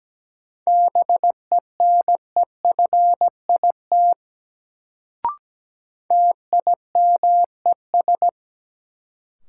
Morse code, heard as a keyed tone, about 17 wpm, 700 Hz. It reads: BENEFIT TIMES
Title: Sentences from Top 1000 Words - Review Part 6 - New Word Drill - 17wpm